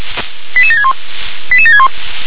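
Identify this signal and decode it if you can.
Five-tone selective call (ZVEI), heard as quick runs of five tones, tones of about 70 ms. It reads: 88761 88761
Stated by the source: ZVEI five-tone selective call received by SDR.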